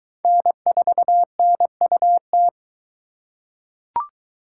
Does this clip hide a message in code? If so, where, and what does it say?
Yes: Morse code from 0.2 to 2.5 s, heard as a keyed tone, about 20 wpm, 700 Hz. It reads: N4NUT